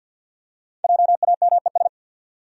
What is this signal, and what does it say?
Morse code, heard as a keyed tone, about 53 wpm, 700 Hz. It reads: JAMES
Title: USA Common Names - Max 5 Letters - 50wpm